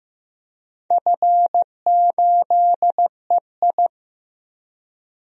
Morse code, heard as a keyed tone, about 15 wpm, 700 Hz. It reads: F8EI
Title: DXCC-100 Call Signs - 4 Characters - 15wpm